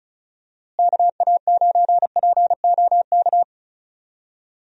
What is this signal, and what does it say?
Morse code, heard as a keyed tone, about 34 wpm, 700 Hz. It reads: KA9POK
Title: USA Call Signs - 6 Characters - Repeated 3x - 35wpm